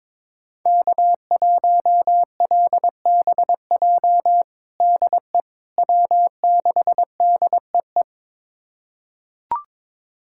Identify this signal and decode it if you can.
Morse code, heard as a keyed tone, about 21 wpm, 700 Hz. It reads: K1LBJ DE W6DEE
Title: USA Callsign DE Callsign - 5 Characters - 22wpm